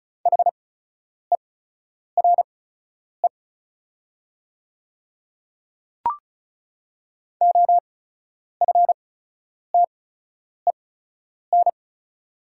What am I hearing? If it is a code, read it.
Morse code, heard as a keyed tone, about 36 wpm, spaced as for 10 wpm, 700 Hz. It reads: HERE OFTEN
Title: 8x Character Spacing - Top 300 Words - 35wpm